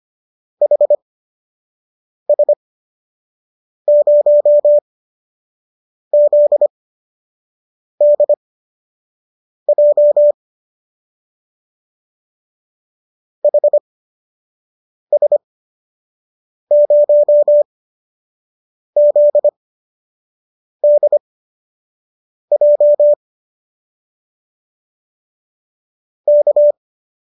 Morse code, heard as a keyed tone, about 25 wpm, spaced as for 6 wpm, 590 Hz. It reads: HS0ZDJ HS0ZDJ K